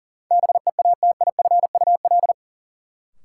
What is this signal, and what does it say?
Morse code, heard as a keyed tone, about 40 wpm, 700 Hz. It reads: BEATIFUL
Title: Sentences from Top 1300 Words - New Word Drill - 40wpm